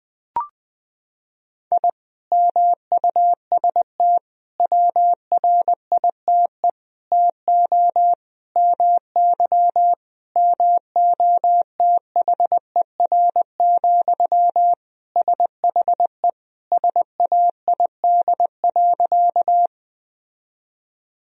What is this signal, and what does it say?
Morse code, heard as a keyed tone, about 21 wpm, 700 Hz. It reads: I MUST WRITE TO MY MOTHER, SHE SAID.
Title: Sentences from Top 1000 Words - Review Part 1 - 20wpm